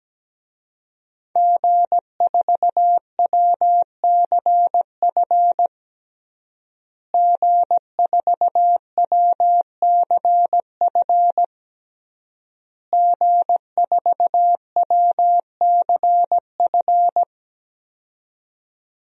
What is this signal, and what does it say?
Morse code, heard as a keyed tone, about 17 wpm, 700 Hz. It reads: G4WCF G4WCF G4WCF
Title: DXCC-100 Call Signs - 5 Characters - Repeated 3x - 17wpm